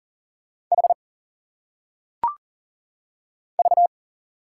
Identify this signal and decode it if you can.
Morse code, heard as a keyed tone, about 40 wpm, 700 Hz. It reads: H V
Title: Single Letters - Phonetic Pronunciation 40wpm